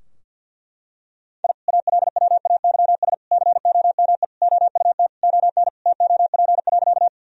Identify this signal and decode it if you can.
Morse code, heard as a keyed tone, about 50 wpm, 700 Hz. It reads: I ALWAYS COME OUT ON TOP.
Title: Sentences from Top 1000 Words - Review Part 2 - 50wpm